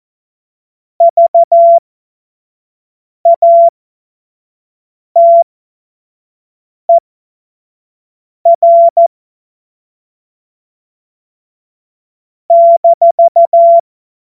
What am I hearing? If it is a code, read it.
Morse code, heard as a keyed tone, about 14 wpm, spaced as for 5 wpm, 680 Hz. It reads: VATER -
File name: Morsekurs / Witze / Witz 14. September 2021, 25 F70